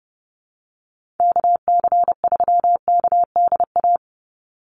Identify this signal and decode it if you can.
Morse code, heard as a keyed tone, about 30 wpm, 700 Hz. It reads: KC3KDA
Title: USA Call Signs 30wpm